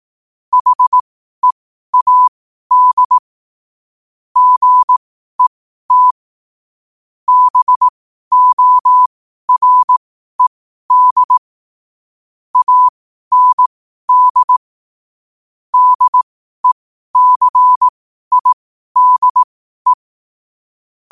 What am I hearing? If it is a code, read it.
Morse code, heard as a keyed tone, about 18 wpm, spaced as for 12 wpm, 1000 Hz. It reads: HEAD GET BORED AND DECIDE
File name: qotdfiles / 12wpm